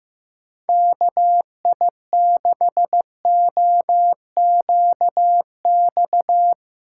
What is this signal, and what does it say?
Morse code, heard as a keyed tone, about 15 wpm, 700 Hz. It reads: KI6OQX
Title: USA Callsign DE Callsign 15wpm